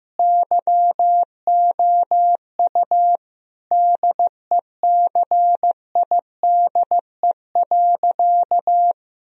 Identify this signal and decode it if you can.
Morse code, heard as a keyed tone, about 15 wpm, 700 Hz. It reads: YOU DECIDE.
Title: Sentences from Top 1000 Words - Review Part 3 - 15wpm